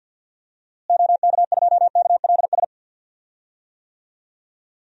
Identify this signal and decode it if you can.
Morse code, heard as a keyed tone, about 51 wpm, 700 Hz. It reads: OK2KLS
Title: DXCC-100 Call Signs - 6 Characters - Repeated 3x - 50wpm